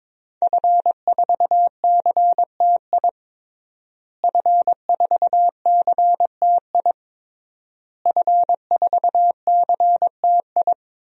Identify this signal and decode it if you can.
Morse code, heard as a keyed tone, about 22 wpm, 700 Hz. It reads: F4CTI F4CTI F4CTI
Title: DXCC-100 Call Signs - 5 Characters - Repeated 3x - 22wpm